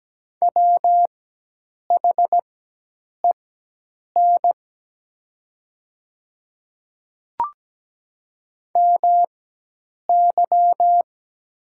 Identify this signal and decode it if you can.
Morse code, heard as a keyed tone, about 17 wpm, spaced as for 8 wpm, 700 Hz. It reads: WHEN MY
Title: 4x Character Spacing - Top 200 Words - 17wpm